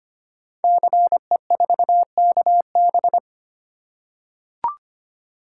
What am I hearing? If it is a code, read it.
Morse code, heard as a keyed tone, about 25 wpm, 700 Hz. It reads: CE4KB
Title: Code Groups - 5 Letter:Numbers - 25wpm